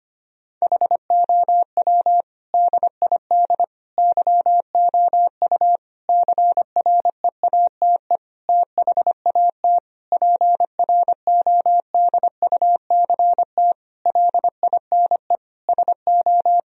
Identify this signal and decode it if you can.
Morse code, heard as a keyed tone, about 25 wpm, 700 Hz. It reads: HOW DID YOU CREATE THAT PRODUCT LINE SO
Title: Sentences from Top 1000 Words - Review Part 3 - 25wpm